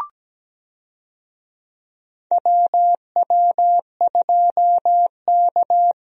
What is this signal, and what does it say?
Morse code, heard as a keyed tone, about 17 wpm, 700 Hz. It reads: WW2K
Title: USA Call Signs - 4 Characters - Repeated 3x - 17wpm